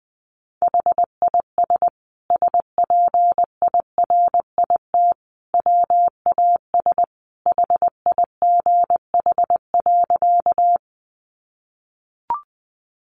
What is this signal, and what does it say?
Morse code, heard as a keyed tone, about 20 wpm, 700 Hz. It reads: HIS SPIRIT WAS HIGH.
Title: Sentences from Top 1400 Words 20wpm